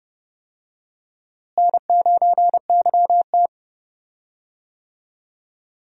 Morse code, heard as a keyed tone, about 32 wpm, 700 Hz. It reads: N9YT
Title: USA Call Signs - 4 Characters - Repeated 2x - 30wpm